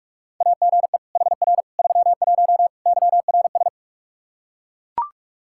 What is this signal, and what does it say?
Morse code, heard as a keyed tone, about 41 wpm, 700 Hz. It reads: AGE HR 31 YRS
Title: Practice QSO - Exchange 3 - More Info - 45wpm